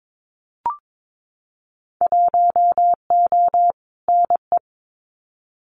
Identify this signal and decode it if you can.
Morse code, heard as a keyed tone, about 22 wpm, 700 Hz. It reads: 1O NE
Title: Winter Field Day Exchanges 22wpm